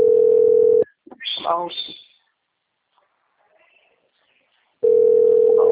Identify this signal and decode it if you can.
Ringback tone (North America), 440 + 480 Hz, with speech.
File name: out- - - - - .25131